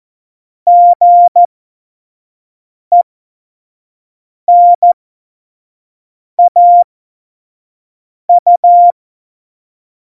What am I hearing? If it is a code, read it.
Morse code, heard as a keyed tone, about 14 wpm, spaced as for 5 wpm, 700 Hz. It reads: GENAU
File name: Morsekurs / Witze / Witz 28. August 2018, 25 F70